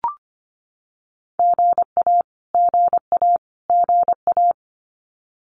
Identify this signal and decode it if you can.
Morse code, heard as a keyed tone, about 25 wpm, 700 Hz. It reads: GA GA GA
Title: Abbreviations Repeated 3x - 25wpm